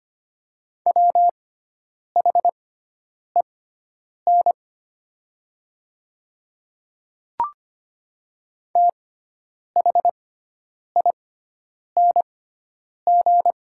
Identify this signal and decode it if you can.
Morse code, heard as a keyed tone, about 25 wpm, spaced as for 9 wpm, 700 Hz. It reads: WHEN THING
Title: 6x Character Spacing - Top 200 Words - 25wpm